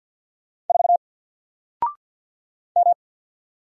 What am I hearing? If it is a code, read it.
Morse code, heard as a keyed tone, about 48 wpm, 700 Hz. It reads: 4 M
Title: Single Letter-Number Alt 50wpm